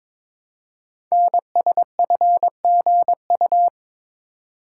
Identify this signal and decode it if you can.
Morse code, heard as a keyed tone, about 22 wpm, 700 Hz. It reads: NSFGU
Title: Code Groups - 5 Letters - 22wpm